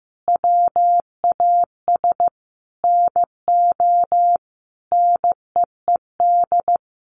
Morse code, heard as a keyed tone, about 15 wpm, 700 Hz. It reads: WAS NO NEED
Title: Sentences from Top 1300 Words 15wpm